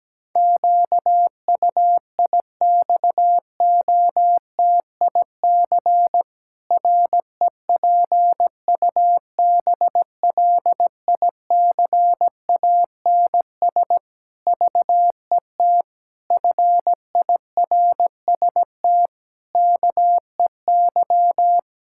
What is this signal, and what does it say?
Morse code, heard as a keyed tone, about 17 wpm, 700 Hz. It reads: QUIXOTIC REPUBLICANS VET FIRST KEY
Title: Pangram Challenge v2 - 17wpm